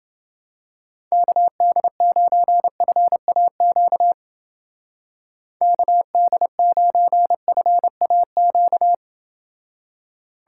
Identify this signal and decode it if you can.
Morse code, heard as a keyed tone, about 29 wpm, 700 Hz. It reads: KD9FAQ KD9FAQ